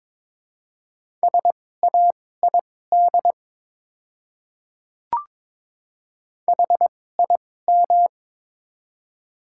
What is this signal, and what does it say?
Morse code, heard as a keyed tone, about 21 wpm, spaced as for 16 wpm, 700 Hz. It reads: SAID HIM
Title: 2x Character Spacing - Top 100 Words - 22wpm